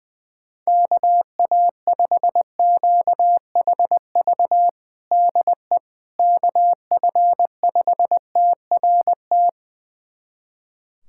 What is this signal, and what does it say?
Morse code, heard as a keyed tone, about 20 wpm, 700 Hz. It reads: KA5QHV DE KF5TRT
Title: USA Callsign DE Callsign - 6 Characters - 20wpm